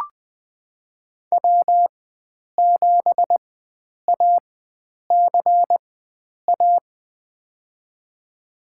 Morse code, heard as a keyed tone, about 20 wpm, spaced as for 9 wpm, 700 Hz. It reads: W7ACA
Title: USA Call Signs - 5 Characters - 4x Character Spacing - 20wpm